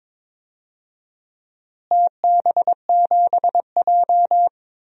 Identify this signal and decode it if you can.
Morse code, heard as a keyed tone, about 22 wpm, 700 Hz. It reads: TB7J
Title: DXCC-100 Call Signs - 4 Characters - Repeated 2x - 22wpm